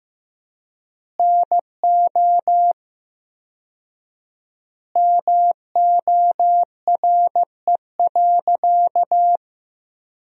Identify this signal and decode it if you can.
Morse code, heard as a keyed tone, about 15 wpm, 700 Hz. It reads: NO MORE.
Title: 4x Word-Spacing - Sentences from Top 200 Words - 15wpm